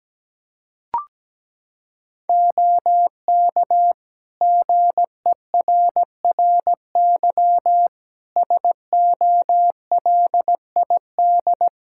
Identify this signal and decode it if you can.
Morse code, heard as a keyed tone, about 17 wpm, 700 Hz. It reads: OK GERRY SOLID
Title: Practice QSO - Exchange 3 - More Info - 17wpm